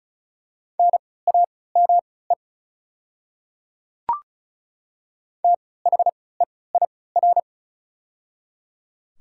Morse code, heard as a keyed tone, about 33 wpm, spaced as for 20 wpm, 700 Hz. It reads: NAME THEIR
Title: 3x Character Spacing - Top 200 Words - 35wpm